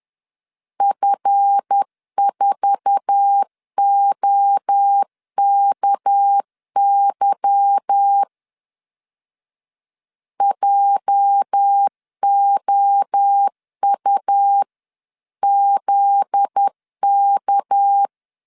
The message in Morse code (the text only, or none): F4OKY JOUZK